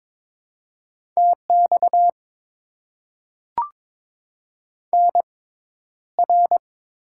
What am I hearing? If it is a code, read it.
Morse code, heard as a keyed tone, about 22 wpm, spaced as for 11 wpm, 700 Hz. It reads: TX NR
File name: Top 90 QSO Elements - 6x Character Spacing - 22wpm